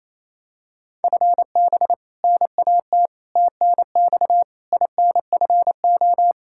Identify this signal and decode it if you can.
Morse code, heard as a keyed tone, about 28 wpm, 700 Hz. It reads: FB NAT TNX INFO
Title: Practice QSO - Exchange 3 - More Info - 28wpm